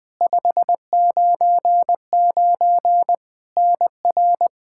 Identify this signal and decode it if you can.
Morse code, heard as a keyed tone, about 20 wpm, 700 Hz. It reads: 599 NR
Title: K1USN SST QSO Practice 20wpm